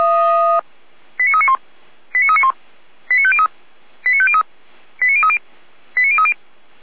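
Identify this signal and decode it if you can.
Five-tone selective call (ZVEI), heard as quick runs of five tones, tones of about 70 ms. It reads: 89391 89391 89503 89503 89030 89030
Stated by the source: Five-tone ZVEI selective call 89125 received on an SDR.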